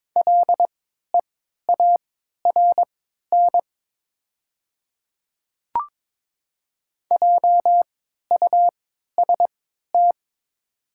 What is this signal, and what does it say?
Morse code, heard as a keyed tone, about 22 wpm, spaced as for 12 wpm, 700 Hz. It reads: LEARN JUST
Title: 3x Character Spacing - Top 300 Words - 22wpm